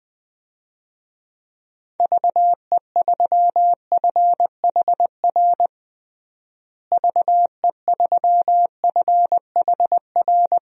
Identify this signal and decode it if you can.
Morse code, heard as a keyed tone, about 21 wpm, 700 Hz. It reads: VE3FHR VE3FHR
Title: DXCC-100 Call Signs - 6 Characters - Repeated 2x - 20wpm